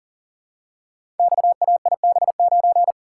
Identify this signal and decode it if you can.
Morse code, heard as a keyed tone, about 39 wpm, 700 Hz. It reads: XAIB9
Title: Code Groups - 5 Letter:Numbers - 40wpm